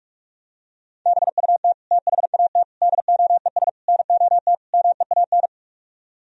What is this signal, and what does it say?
Morse code, heard as a keyed tone, about 49 wpm, 700 Hz. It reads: BUT THAT DOES NOT MEAN